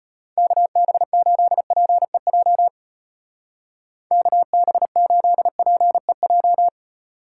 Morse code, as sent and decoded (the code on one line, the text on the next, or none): -.- -... ---.. .--. .--- / -.- -... ---.. .--. .---
KB8PEJ KB8PEJ